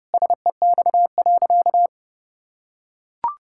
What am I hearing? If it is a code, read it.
Morse code, heard as a keyed tone, about 30 wpm, 700 Hz. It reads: SEX.
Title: Sentences from Top 1600 Words 30wpm